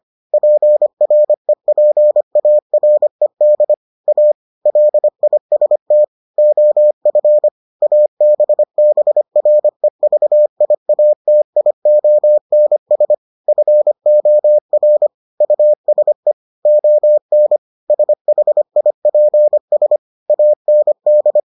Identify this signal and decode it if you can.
Morse code, heard as a keyed tone, about 25 wpm, 600 Hz. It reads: PREPARED A LIST OF ABBREVIATIONS FOR USE ON SHIPS AND